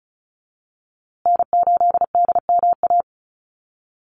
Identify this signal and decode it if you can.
Morse code, heard as a keyed tone, about 34 wpm, 700 Hz. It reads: N8DMA
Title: USA Call Signs Repeated 3x 35wpm